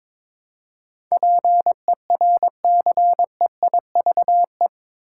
Morse code, heard as a keyed tone, about 22 wpm, 700 Hz. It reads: PERCEIVE
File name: practice / 22wpm / Eight Letter Words - 22wpm